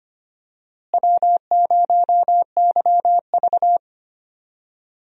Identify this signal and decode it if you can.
Morse code, heard as a keyed tone, about 25 wpm, 700 Hz. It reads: W0YV
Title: USA Call Signs - 4 Characters - Repeated 2x - 25wpm